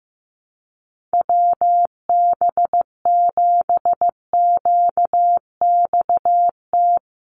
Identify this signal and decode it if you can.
Morse code, heard as a keyed tone, about 15 wpm, 700 Hz. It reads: WB7QXT
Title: USA Call Signs 15wpm